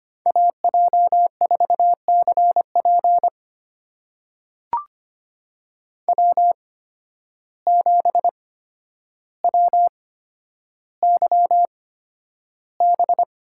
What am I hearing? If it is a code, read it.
Morse code, heard as a keyed tone, about 25 wpm, spaced as for 11 wpm, 700 Hz. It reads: AJ4CP W7WYB
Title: USA Call Signs - 5 Characters - 8x Character Spacing - 25wpm